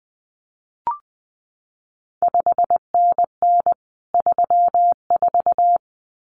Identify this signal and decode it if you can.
Morse code, heard as a keyed tone, about 19 wpm, 700 Hz. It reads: 5NN 34
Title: CQ WW DX Contest Exchange Encore - 20wpm